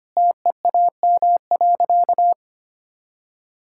Morse code, heard as a keyed tone, about 25 wpm, 700 Hz. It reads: TEAM.
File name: practice / 25wpm / Sentences with 3-Word Phrases - 25wpm